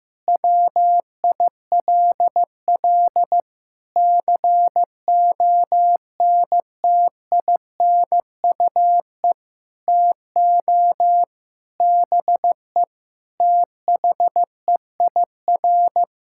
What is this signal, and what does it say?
Morse code, heard as a keyed tone, about 16 wpm, 700 Hz. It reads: WILL CONTINUE TO BE THEIR
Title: Sentences from Top 1600 Words 15wpm